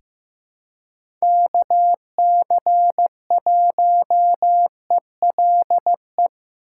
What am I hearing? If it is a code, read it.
Morse code, heard as a keyed tone, about 15 wpm, 700 Hz. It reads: KC1ELE
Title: USA Call Signs - 6 Characters - 3x Character Spacing - 15wpm